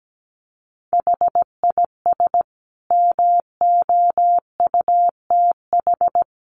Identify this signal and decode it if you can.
Morse code, heard as a keyed tone, about 17 wpm, 700 Hz. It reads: HIS MOUTH
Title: A Sound Of Thunder 17wpm